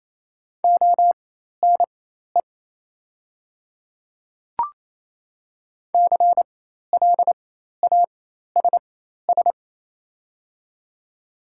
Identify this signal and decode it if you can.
Morse code, heard as a keyed tone, about 28 wpm, spaced as for 13 wpm, 690 Hz. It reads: ONE CLASS
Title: 4x Character Spacing - Top 500 Words - 28wpm